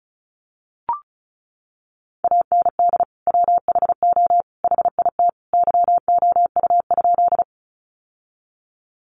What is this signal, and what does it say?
Morse code, heard as a keyed tone, about 34 wpm, 700 Hz. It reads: AND WHO HIT YOU?
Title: Sentences from Top 1200 Words 35wpm